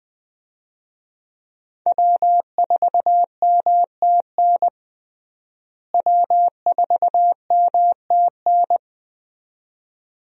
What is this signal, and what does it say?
Morse code, heard as a keyed tone, about 21 wpm, 700 Hz. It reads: W4MTN W4MTN
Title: USA Call Signs - 5 Characters - Repeated 2x - 20wpm